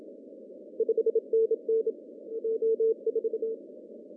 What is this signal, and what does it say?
Morse code, heard as a keyed tone, about 30 wpm, 470 Hz. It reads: 5NN O4
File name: CW / cqww-cw-k5tr